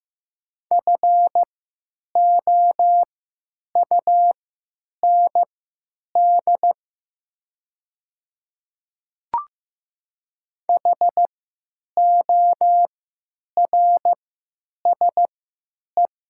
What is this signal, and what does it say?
Morse code, heard as a keyed tone, about 15 wpm, spaced as for 8 wpm, 700 Hz. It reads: FOUND HORSE
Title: 3x Character Spacing - Top 500 Words - 15wpm